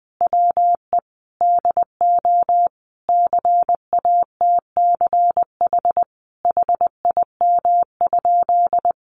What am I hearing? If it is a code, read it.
Morse code, heard as a keyed tone, about 19 wpm, 700 Hz. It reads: WE DO CATCH HIM?
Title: Sentences from Top 1500 Words 20wpm